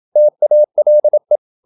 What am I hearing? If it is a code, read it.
Morse code, heard as a keyed tone, about 27 wpm, 600 Hz. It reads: TALE
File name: Ferie QRQ 27wpm